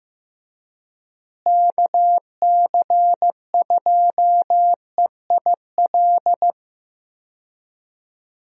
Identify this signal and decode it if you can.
Morse code, heard as a keyed tone, about 15 wpm, 700 Hz. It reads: KC2EIL
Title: USA Call Signs - 6 Characters - Repeated 2x - 15wpm